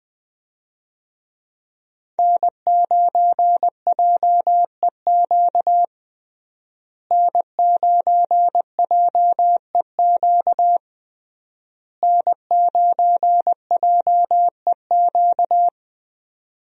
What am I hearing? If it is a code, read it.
Morse code, heard as a keyed tone, about 20 wpm, 700 Hz. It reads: N9JEQ N9JEQ N9JEQ
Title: USA Call Signs - 5 Characters - Repeated 3x - 20wpm